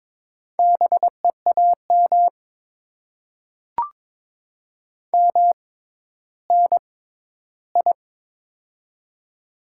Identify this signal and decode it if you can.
Morse code, heard as a keyed tone, about 22 wpm, spaced as for 12 wpm, 700 Hz. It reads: BEAM MNI
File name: Top 90 QSO Elements - 6x Character Spacing - 22wpm